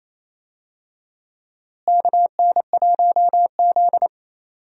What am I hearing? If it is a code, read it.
Morse code, heard as a keyed tone, about 28 wpm, 700 Hz. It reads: KN1Z
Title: USA Call Signs - 4 Characters - Repeated 3x - 28wpm